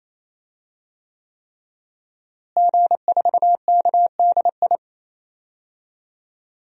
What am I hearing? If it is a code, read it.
Morse code, heard as a keyed tone, about 28 wpm, 700 Hz. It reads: G4KDI
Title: DXCC-100 Call Signs - 5 Characters - Repeated 3x - 28wpm